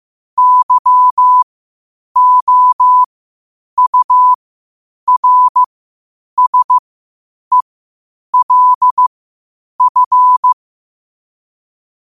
Morse code, heard as a keyed tone, about 15 wpm, spaced as for 8 wpm, 1000 Hz. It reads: YOURSELF
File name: qotdfiles / 8wpm